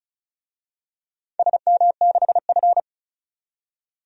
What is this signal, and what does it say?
Morse code, heard as a keyed tone, about 36 wpm, 700 Hz. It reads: SM6F
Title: DXCC-100 Call Signs - 4 Characters - 35wpm